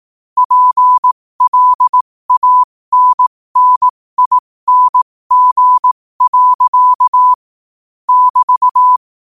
Morse code, heard as a keyed tone, about 18 wpm, spaced as for 15 wpm, 1000 Hz. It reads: PLANNING. <BT>
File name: qotdfiles / 15wpm